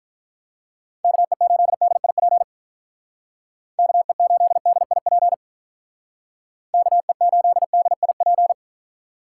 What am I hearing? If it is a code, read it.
Morse code, heard as a keyed tone, about 45 wpm, 700 Hz. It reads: KE8DIP KE8DIP KE8DIP